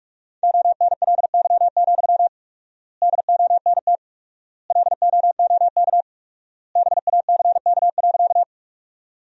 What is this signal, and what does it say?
Morse code, heard as a keyed tone, about 46 wpm, 700 Hz. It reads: ONLY, DONT LOOK BACK.
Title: 4x Word-Spacing - Sentences from Top 300 Words - 45wpm